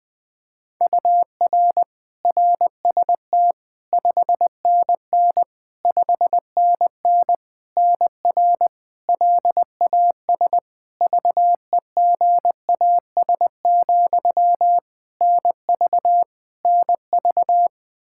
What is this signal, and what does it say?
Morse code, heard as a keyed tone, about 20 wpm, 700 Hz. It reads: UR RST 5NN 5NN NR LAS VEGAS, NV NV
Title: K1USN SST QSO Practice 20wpm